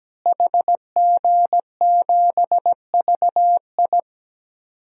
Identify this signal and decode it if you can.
Morse code, heard as a keyed tone, about 17 wpm, 700 Hz. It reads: HG7VI
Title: DXCC-100 Call Signs - 5 Characters - Repeated 3x - 17wpm